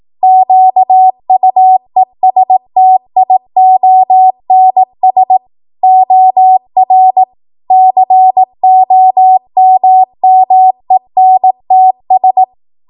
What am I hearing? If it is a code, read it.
Morse code, heard as a keyed tone, about 18 wpm, 750 Hz. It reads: QUESTIONS OR COMMENTS
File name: Archive / 18 WPM / 18WPM